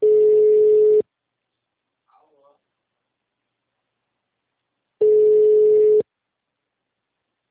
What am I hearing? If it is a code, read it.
Ringback tone (Europe, 425 Hz).
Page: Barueri - SP